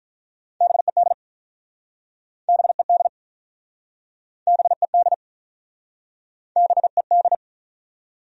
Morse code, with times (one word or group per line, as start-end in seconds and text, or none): 0.6-1.1 BED
2.5-3.1 BED
4.5-5.1 BED
6.6-7.3 BED